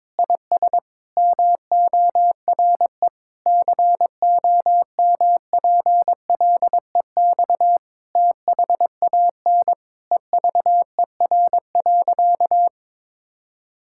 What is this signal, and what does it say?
Morse code, heard as a keyed tone, about 22 wpm, 700 Hz. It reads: IS MORE COMPLEX THAN EVER.
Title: Sentences from Top 1000 Words - Review Part 6 - 22wpm